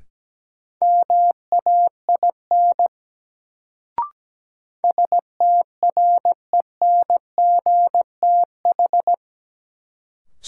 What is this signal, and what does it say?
Morse code, heard as a keyed tone, about 17 wpm, 700 Hz. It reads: MAIN STRENGTH
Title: Sentences from Top 1300 Words - New Word Drill Alt - 17wpm